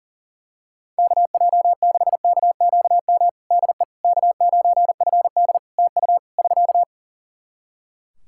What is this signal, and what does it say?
Morse code, heard as a keyed tone, about 40 wpm, 700 Hz. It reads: KJ6KQM DE K9FD TU <SK>
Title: K1USN SST QSO Practice 40wpm